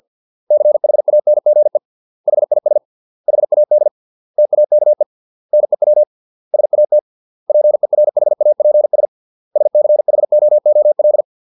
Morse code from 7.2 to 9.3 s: PERHAPS